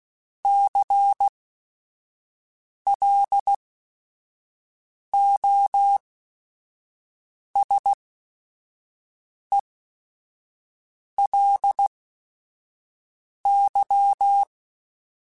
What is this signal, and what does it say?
Morse code, heard as a keyed tone, about 16 wpm, spaced as for 5 wpm, 800 Hz. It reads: CLOSELY
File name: Technical Subjects / Technical Subjects-mp3s / OR Gate @5